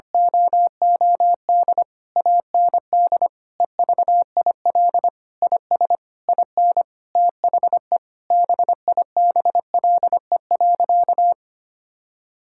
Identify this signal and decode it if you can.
Morse code, heard as a keyed tone, about 25 wpm, 700 Hz. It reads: OOD AND EVIL IS IN THE BIBLE.
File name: practice / 25wpm / Sentences with 9-Word Phrases - 25wpm